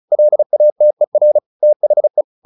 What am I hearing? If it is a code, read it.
Morse code, heard as a keyed tone, about 34 wpm, 600 Hz. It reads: LATER THE
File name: the old man and the sea / the old man and the sea 1 35wpm